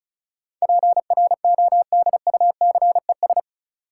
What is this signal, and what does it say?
Morse code, heard as a keyed tone, about 34 wpm, 700 Hz. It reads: PRODUCES
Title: Sentences from Top 1000 Words - Review Part 6 - New Word Drill - 35wpm